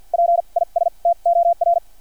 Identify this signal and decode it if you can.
Morse code, heard as a keyed tone, about 50 wpm, spaced as for 38 wpm, 690 Hz. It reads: PISTOR
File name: archived / CRSnet